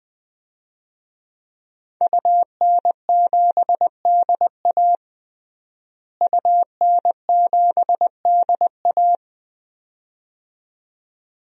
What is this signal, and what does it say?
Morse code, heard as a keyed tone, about 21 wpm, 700 Hz. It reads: UN7DA UN7DA
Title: DXCC-100 Call Signs - 5 Characters - Repeated 2x - 20wpm